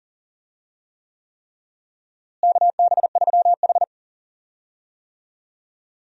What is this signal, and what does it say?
Morse code, heard as a keyed tone, about 40 wpm, 700 Hz. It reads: KB3H
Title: USA Call Signs - 4 Characters - Repeated 3x - 40wpm